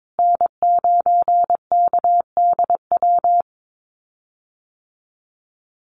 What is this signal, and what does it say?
Morse code, heard as a keyed tone, about 22 wpm, 700 Hz. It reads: N9KDW
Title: USA Call Signs Repeated 3x 22wpm